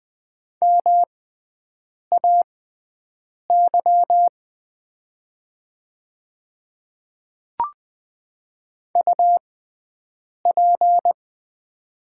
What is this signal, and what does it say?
Morse code, heard as a keyed tone, about 20 wpm, spaced as for 7 wpm, 700 Hz. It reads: MAY UP
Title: 6x Character Spacing - Top 100 Words - 20wpm